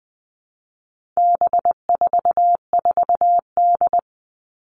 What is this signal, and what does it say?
Morse code, heard as a keyed tone, about 20 wpm, 700 Hz. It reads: B44D
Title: All Cut Numbers - 4-Digit 20wpm